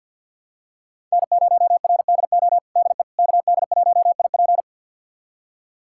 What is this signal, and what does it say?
Morse code, heard as a keyed tone, about 52 wpm, 690 Hz. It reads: N0RDO DE KD1IP
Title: USA Callsign DE Callsign - 5 Characters - 50wpm